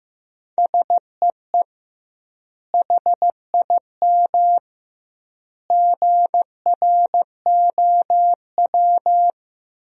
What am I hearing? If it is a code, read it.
Morse code, heard as a keyed tone, about 15 wpm, 700 Hz. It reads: SEE HIM GROW
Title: 2x Word-Spacing - Sentences from Top 300 Words - 15wpm